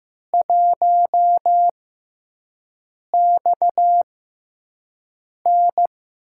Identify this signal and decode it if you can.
Morse code, heard as a keyed tone, about 15 wpm, spaced as for 5 wpm, 700 Hz. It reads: 1XN
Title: USA Call Signs - 6x Character Spacing - 15wpm